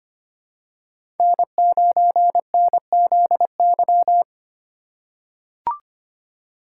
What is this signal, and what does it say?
Morse code, heard as a keyed tone, about 25 wpm, 700 Hz. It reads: N9NZY
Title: USA Call Signs - 5 Characters - 2x Character Spacing - 25wpm